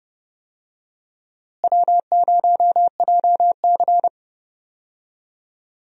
Morse code, heard as a keyed tone, about 30 wpm, 700 Hz. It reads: W0JC
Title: USA Call Signs - 4 Characters - Repeated 3x - 30wpm